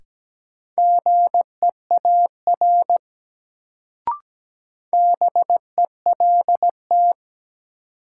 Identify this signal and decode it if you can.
Morse code, heard as a keyed tone, about 17 wpm, 700 Hz. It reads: GEAR BELT
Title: Sentences from Top 1400 Words - New Word Drill Alt - 17wpm